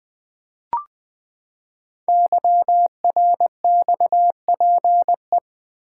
Morse code, heard as a keyed tone, about 20 wpm, 700 Hz. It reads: YRXPE
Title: Code Groups - 5 Letters - 20wpm